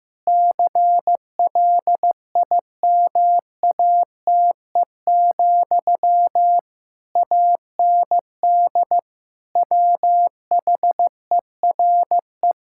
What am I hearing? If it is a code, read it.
Morse code, heard as a keyed tone, about 15 wpm, 700 Hz. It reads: CLIMATE, AND WHERE
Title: Sentences from Top 1600 Words 15wpm